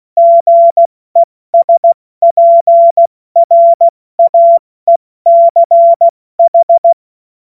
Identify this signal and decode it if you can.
Morse code, heard as a keyed tone, about 16 wpm, spaced as for 14 wpm, 680 Hz. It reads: GESPRAECH
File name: Morsekurs / Witze / Witz 14. September 2021, 70 F80